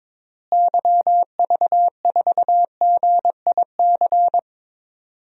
Morse code, heard as a keyed tone, about 22 wpm, 700 Hz. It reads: YV4GIC
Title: DXCC-100 Call Signs - 6 Characters - Repeated 3x - 22wpm